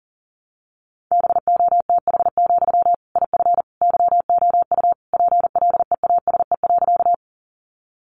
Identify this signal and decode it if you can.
Morse code, heard as a keyed tone, about 40 wpm, 700 Hz. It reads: BOTH, IF YOU PLEASE.
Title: Sentences from Top 1500 Words 40wpm